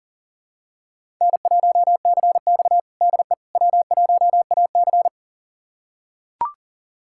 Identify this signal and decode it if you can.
Morse code, heard as a keyed tone, about 37 wpm, 700 Hz. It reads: N1CX DE W1AC
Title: USA Callsign DE Callsign - 4 Characters - 40wpm